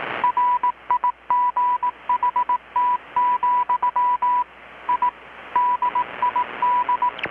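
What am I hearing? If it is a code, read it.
Morse code, heard as a keyed tone, about 18 wpm, 980 Hz. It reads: RIGHT, I DID
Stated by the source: Morse from KPH's Night of Nights XX maritime radiotelegraphy event, received in Chiapas, Mexico about 3,800 km away on a Kenwood R-600 with a grounded 10 m long-wire antenna.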